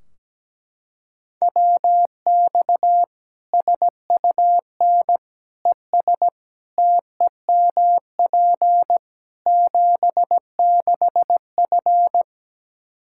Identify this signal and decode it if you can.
Morse code, heard as a keyed tone, about 17 wpm, 700 Hz. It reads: WX SUN ES TEMP 76F
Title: Practice QSO - Exchange 2 - Info - 17wpm